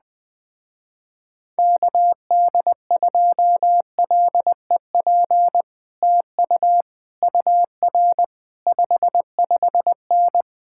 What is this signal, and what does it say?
Morse code, heard as a keyed tone, about 20 wpm, 700 Hz. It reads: KD2LEP TU UR 55N
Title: Speed-Racing Common QSO Phrases - 17wpm